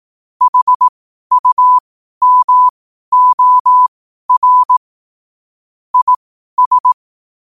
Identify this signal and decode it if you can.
Morse code, heard as a keyed tone, about 18 wpm, spaced as for 12 wpm, 1000 Hz. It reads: HUMOR IS
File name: qotdfiles / 12wpm